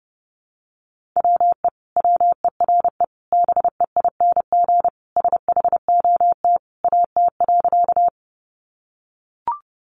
Morse code, heard as a keyed tone, about 28 wpm, 700 Hz. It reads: WE WERE BEING SHOT AT.
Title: Sentences from Top 1100 Words 30wpm